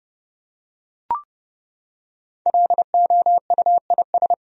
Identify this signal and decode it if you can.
Morse code, heard as a keyed tone, about 30 wpm, 700 Hz. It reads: LOUIS
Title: USA Common Names - 30wpm